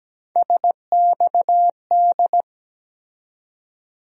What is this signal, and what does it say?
Morse code, heard as a keyed tone, about 17 wpm, 700 Hz. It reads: SXD
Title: USA Call Sign Postfixes - 3 Letters 17wpm